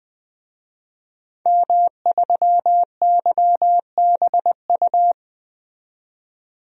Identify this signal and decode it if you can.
Morse code, heard as a keyed tone, about 21 wpm, 700 Hz. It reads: M3YBU
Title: DXCC-100 Call Signs - 5 Characters - 20wpm